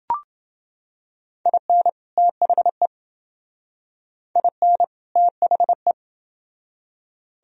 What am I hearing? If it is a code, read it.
Morse code, heard as a keyed tone, about 30 wpm, 700 Hz. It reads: IN THE IN THE